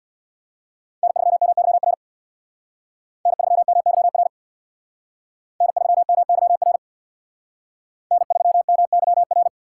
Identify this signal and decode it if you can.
Morse code, heard as a keyed tone, about 62 wpm, 700 Hz. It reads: N3MYR N3MYR N3MYR N3MYR
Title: Speed Racing - USA 1x3 Call Signs 50wpm